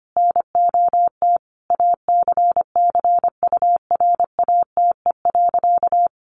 Morse code, heard as a keyed tone, about 25 wpm, 700 Hz. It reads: NOT ACCURATE.
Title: Sentences from Top 1200 Words 25wpm